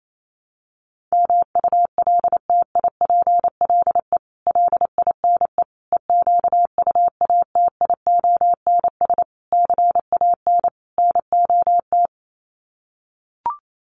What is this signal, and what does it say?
Morse code, heard as a keyed tone, about 28 wpm, 700 Hz. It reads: MULTIPLE LINE EQUATIONS CAN NOT
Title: Sets of 5 Words 28wpm